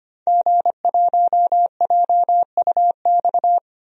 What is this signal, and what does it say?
Morse code, heard as a keyed tone, about 25 wpm, 700 Hz. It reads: G1JUX
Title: DXCC-100 Call Signs - 5 Characters - 25wpm